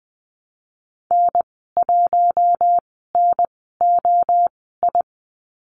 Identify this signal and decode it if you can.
Morse code, heard as a keyed tone, about 20 wpm, spaced as for 14 wpm, 700 Hz. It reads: N1NOI
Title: 076 Lesson 13g - Letter L - Call Signs 20wpm